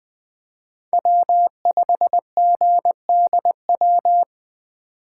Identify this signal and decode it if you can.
Morse code, heard as a keyed tone, about 21 wpm, 700 Hz. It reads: W5GDW
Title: 2023 SOTA Chaser Call Signs - 20wpm